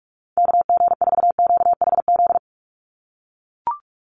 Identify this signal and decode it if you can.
Morse code, heard as a keyed tone, about 44 wpm, 700 Hz. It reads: KG4QHZ
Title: USA Call Signs 45wpm